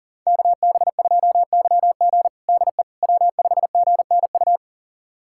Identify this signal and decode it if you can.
Morse code, heard as a keyed tone, about 40 wpm, 700 Hz. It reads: KB2YG DE W5GNU
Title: USA Callsign DE Callsign - 5 Characters - 40wpm